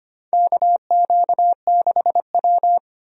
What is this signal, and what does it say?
Morse code, heard as a keyed tone, about 25 wpm, 700 Hz. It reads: KQ6W
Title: USA Call Signs - 4 Characters - 25wpm